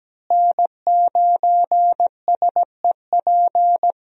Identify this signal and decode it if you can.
Morse code, heard as a keyed tone, about 17 wpm, 700 Hz. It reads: N9SEP
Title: USA Call Signs - 5 Characters - Repeated 2x - 17wpm